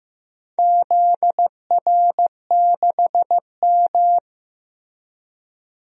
Morse code, heard as a keyed tone, about 15 wpm, 700 Hz. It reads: ZR6M
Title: DXCC-100 Call Signs - 4 Characters - Repeated 2x - 15wpm